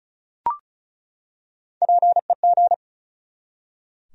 Morse code, heard as a keyed tone, about 36 wpm, 700 Hz. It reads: PEG
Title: USA Common Names - 35wpm